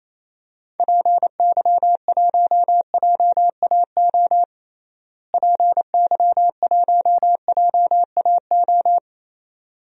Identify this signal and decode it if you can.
Morse code, heard as a keyed tone, about 27 wpm, 700 Hz. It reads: PY1JAO PY1JAO
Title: DXCC-100 Call Signs - 6 Characters - Repeated 2x - 28wpm